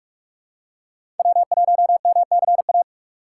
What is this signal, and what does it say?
Morse code, heard as a keyed tone, about 42 wpm, 700 Hz. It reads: W1MCA
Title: USA Call Signs - 5 Characters - 4x Character Spacing - 45wpm